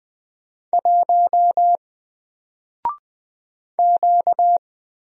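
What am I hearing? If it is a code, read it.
Morse code, heard as a keyed tone, about 20 wpm, 700 Hz. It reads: 1 Q
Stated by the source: Single letter and number Morse drill at 20 wpm, from N4LCD.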